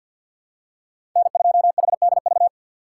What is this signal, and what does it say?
Morse code, heard as a keyed tone, about 53 wpm, 700 Hz. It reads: N2HDV
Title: USA Call Signs - 2x Character Spacing - 50wpm